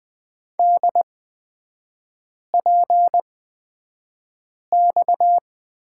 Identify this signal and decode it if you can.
Morse code, heard as a keyed tone, about 20 wpm, spaced as for 5 wpm, 690 Hz. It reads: DPX
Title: 229 Lesson 37e - Letter x - Three Characters 20wpm